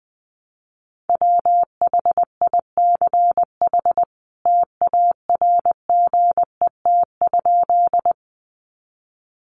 Morse code, heard as a keyed tone, about 19 wpm, 700 Hz. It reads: WHICH TARGET?
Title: Sentences from Top 1200 Words 20wpm